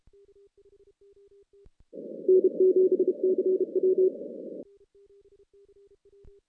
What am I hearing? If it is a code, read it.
Morse code, heard as a keyed tone, about 32 wpm, 370 Hz. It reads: N7CW
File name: 2016 JAN CW NAQP / 2016 Jan naqp k5ot - 0-10